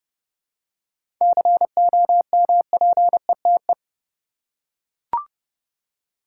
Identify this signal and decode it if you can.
Morse code, heard as a keyed tone, about 32 wpm, 700 Hz. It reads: COMPETE
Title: Seven Letter Words - 30wpm